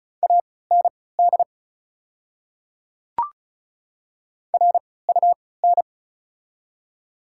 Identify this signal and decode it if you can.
Morse code, heard as a keyed tone, about 36 wpm, spaced as for 20 wpm, 700 Hz. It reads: AND RUN
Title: 3x Character Spacing - Top 500 Words - 35wpm